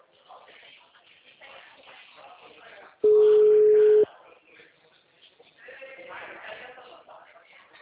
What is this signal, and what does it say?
Ringback tone (Europe, 425 Hz), with speech.